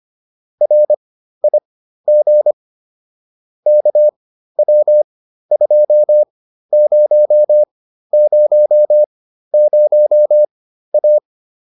Morse code, heard as a keyed tone, about 25 wpm, spaced as for 13 wpm, 600 Hz. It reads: RIG KW2000A